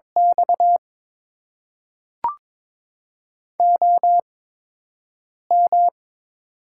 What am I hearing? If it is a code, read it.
Morse code, heard as a keyed tone, about 22 wpm, spaced as for 6 wpm, 700 Hz. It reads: X OM
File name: Top 90 QSO Elements - 8x Character Spacing - 22wpm